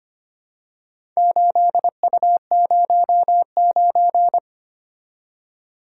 Morse code, heard as a keyed tone, about 25 wpm, 700 Hz. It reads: 8U09
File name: Four Letter-Number Combinations - 25wpm